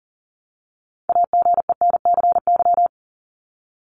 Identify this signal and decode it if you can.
Morse code, heard as a keyed tone, about 40 wpm, 700 Hz. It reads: AGENCY
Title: Top 1001-1250 Words 40wpm